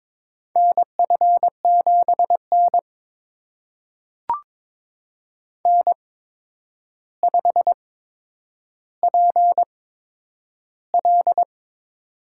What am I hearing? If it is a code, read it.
Morse code, heard as a keyed tone, about 22 wpm, spaced as for 9 wpm, 700 Hz. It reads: NF7N N5PL
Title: USA Call Signs - 4 Characters - 8x Character Spacing - 22wpm